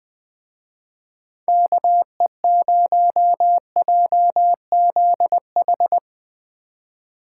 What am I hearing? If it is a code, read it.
Morse code, heard as a keyed tone, about 20 wpm, 700 Hz. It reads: KE0JZH